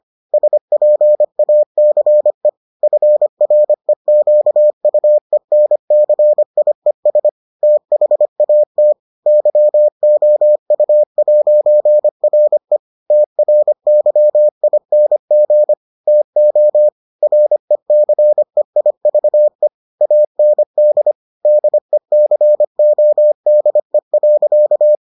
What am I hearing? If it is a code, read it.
Morse code, heard as a keyed tone, about 25 wpm, 600 Hz. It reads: SPACE FREQUENCIES THAT YOU'RE TRYING TO RECEIVE AND DECODE.